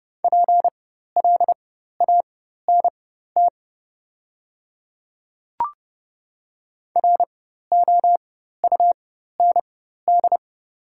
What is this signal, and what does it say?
Morse code, heard as a keyed tone, about 30 wpm, spaced as for 14 wpm, 700 Hz. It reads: PLANT ROUND
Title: 4x Character Spacing - Top 300 Words - 30wpm